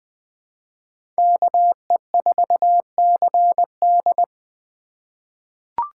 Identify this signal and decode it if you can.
Morse code, heard as a keyed tone, about 20 wpm, 700 Hz. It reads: KE4CD
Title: USA Call Signs - 5 Characters - 3x Character Spacing - 20wpm